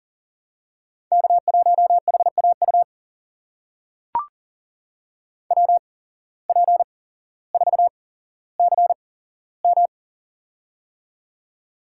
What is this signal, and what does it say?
Morse code, heard as a keyed tone, about 40 wpm, spaced as for 17 wpm, 700 Hz. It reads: K1HAU WP4CM
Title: USA Call Signs - 8x Character Spacing - 40wpm